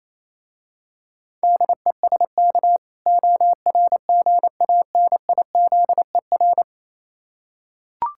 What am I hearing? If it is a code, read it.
Morse code, heard as a keyed tone, about 27 wpm, 700 Hz. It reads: DESK ORGANIZER
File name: Easy Words - Speed-Racing with Hints - 22wpm